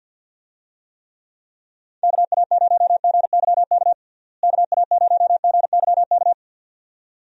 Morse code, heard as a keyed tone, about 49 wpm, 700 Hz. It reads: KA0GYK KA0GYK
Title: USA Call Signs - 6 Characters - Repeated 2x - 50wpm